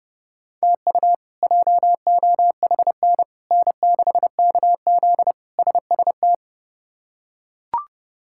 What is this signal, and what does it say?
Morse code, heard as a keyed tone, about 32 wpm, 700 Hz. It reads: TU JOHN N6KZ SST
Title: K1USN SST QSO Practice - 30wpm